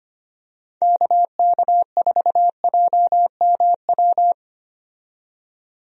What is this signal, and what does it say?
Morse code, heard as a keyed tone, about 25 wpm, 700 Hz. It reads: KK4JMW
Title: USA Call Signs - 6 Characters - 25wpm